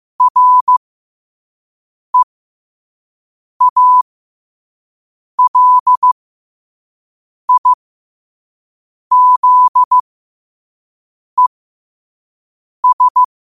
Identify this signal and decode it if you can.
Morse code, heard as a keyed tone, about 15 wpm, spaced as for 5 wpm, 1000 Hz. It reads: REALIZES
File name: qotdfiles / 5wpm